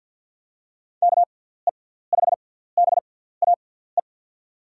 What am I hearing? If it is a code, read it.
Morse code, heard as a keyed tone, about 46 wpm, spaced as for 17 wpm, 700 Hz. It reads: KE5BAE
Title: USA Call Signs - 6 Characters - 6x Character Spacing - 50wpm